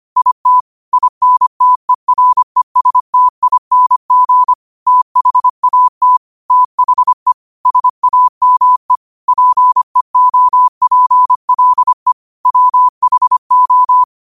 Morse code, heard as a keyed tone, about 25 wpm, 1000 Hz. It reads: IT INTERESTING THAT THE SAME PEOPLE WHO